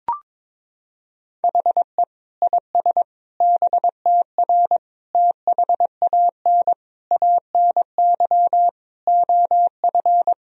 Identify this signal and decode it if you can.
Morse code, heard as a keyed tone, about 23 wpm, 700 Hz. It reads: HE IS BTR THAN ANY OF
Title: Sentences from Top 400 Words and Abbreviations 22wpm